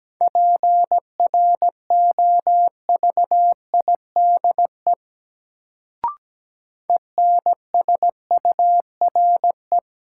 Morse code, heard as a keyed tone, about 17 wpm, 700 Hz. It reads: PROVIDE ENSURE
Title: Sentences from Top 1100 Words - New Word Drill Alt - 17wpm